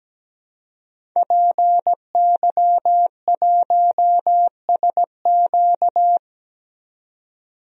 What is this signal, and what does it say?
Morse code, heard as a keyed tone, about 17 wpm, 700 Hz. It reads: PY1SQ
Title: DXCC-100 Call Signs - 5 Characters - 17wpm